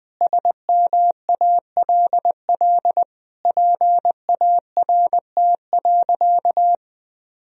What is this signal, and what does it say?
Morse code, heard as a keyed tone, about 21 wpm, 700 Hz. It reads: SMALL PART.
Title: Sentences from Top 1000 Words - Review Part 1 - 20wpm